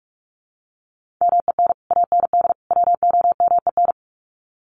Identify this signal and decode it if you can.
Morse code, heard as a keyed tone, about 44 wpm, 700 Hz. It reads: MEN AND WOMEN
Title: Sets of 3 Words 45wpm